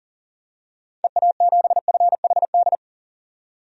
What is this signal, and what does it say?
Morse code, heard as a keyed tone, about 40 wpm, 700 Hz. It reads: EA7FHD
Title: DXCC-100 Call Signs - 6 Characters - 40wpm